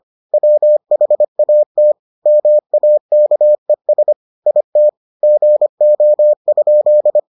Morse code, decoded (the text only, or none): WHAT MAKES IT GO?